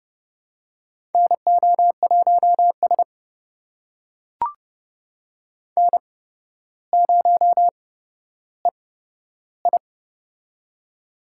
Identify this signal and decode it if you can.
Morse code, heard as a keyed tone, about 30 wpm, spaced as for 13 wpm, 700 Hz. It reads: NO1S N0EI
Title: USA Call Signs - 4 Characters - 8x Character Spacing - 30wpm